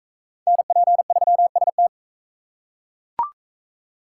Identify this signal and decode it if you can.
Morse code, heard as a keyed tone, about 40 wpm, 700 Hz. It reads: NP3ST